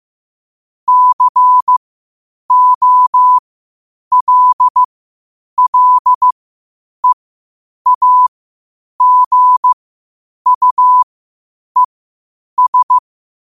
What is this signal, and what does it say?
Morse code, heard as a keyed tone, about 15 wpm, spaced as for 8 wpm, 1000 Hz. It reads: COLLEAGUES
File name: qotdfiles / 8wpm